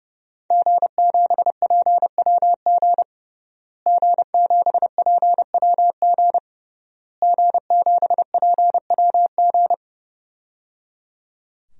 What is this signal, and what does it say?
Morse code, heard as a keyed tone, about 29 wpm, 700 Hz. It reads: G7PWG G7PWG G7PWG
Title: DXCC-100 Call Signs - 5 Characters - Repeated 3x - 30wpm